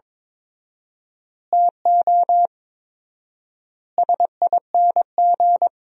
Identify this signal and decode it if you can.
Morse code, heard as a keyed tone, about 22 wpm, 700 Hz. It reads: TO SING
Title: 4x Word-Spacing - Sentences from Top 400 Words - 22wpm